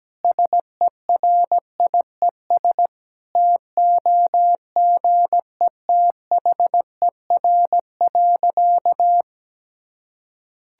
Morse code, heard as a keyed tone, about 17 wpm, 700 Hz. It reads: SERIES TOGETHER.
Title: Sentences from Top 1000 Words - Review Part 5 - 17wpm